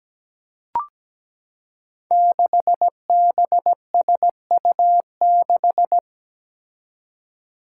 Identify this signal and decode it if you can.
Morse code, heard as a keyed tone, about 17 wpm, 700 Hz. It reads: 6BSU6
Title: Five Challenging Letter-Number Combinations 17wpm